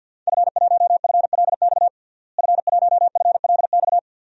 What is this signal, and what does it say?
Morse code, heard as a keyed tone, about 49 wpm, 700 Hz. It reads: F1FLX F1FLX
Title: DXCC-100 Call Signs - 5 Characters - Repeated 3x - 50wpm